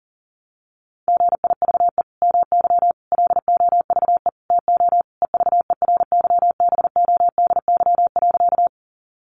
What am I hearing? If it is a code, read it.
Morse code, heard as a keyed tone, about 40 wpm, 700 Hz. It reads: GIVE MY LOVE TO EVERYBODY.
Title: Sentences from Top 1400 Words 40wpm